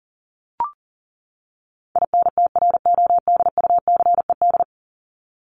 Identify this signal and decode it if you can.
Morse code, heard as a keyed tone, about 40 wpm, 700 Hz. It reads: INTRODUCED
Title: ED Suffix - Top 100 Words with Base Words 40wpm